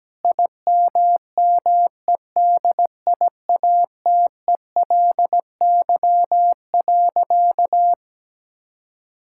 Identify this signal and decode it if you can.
Morse code, heard as a keyed tone, about 17 wpm, 700 Hz. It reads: IMMEDIATELY.
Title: Sentences from Top 1600 Words 17wpm